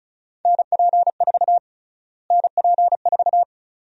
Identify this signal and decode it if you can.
Morse code, heard as a keyed tone, about 36 wpm, 700 Hz. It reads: NP4 NP4
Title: USA Call Sign Prefixes - Repeated 2x - 35wpm